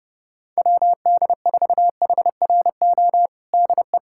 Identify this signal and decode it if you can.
Morse code, heard as a keyed tone, about 31 wpm, 700 Hz. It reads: WD4HRO DE
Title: USA Callsign DE Callsign 30wpm